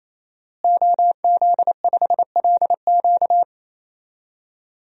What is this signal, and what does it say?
Morse code, heard as a keyed tone, about 28 wpm, 700 Hz. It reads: OZ5LQ
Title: DXCC-100 Call Signs - 5 Characters - Repeated 3x - 28wpm